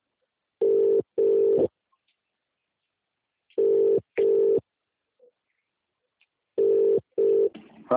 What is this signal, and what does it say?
Ringback tone (UK), 400 + 450 Hz.